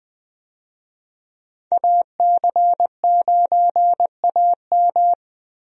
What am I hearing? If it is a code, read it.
Morse code, heard as a keyed tone, about 21 wpm, 700 Hz. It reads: AC9AM
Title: USA Call Signs - 5 Characters - Repeated 2x - 20wpm